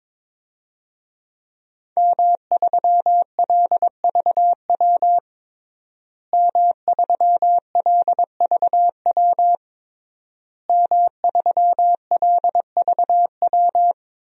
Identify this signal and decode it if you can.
Morse code, heard as a keyed tone, about 22 wpm, 700 Hz. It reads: M3LVW M3LVW M3LVW
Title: DXCC-100 Call Signs - 5 Characters - Repeated 3x - 22wpm